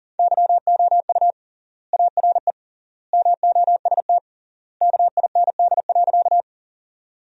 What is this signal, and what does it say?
Morse code, heard as a keyed tone, about 40 wpm, 700 Hz. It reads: YOU ARE MOST KIND.
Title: 3x Word-Spacing - Sentences from Top 200 Words - 40wpm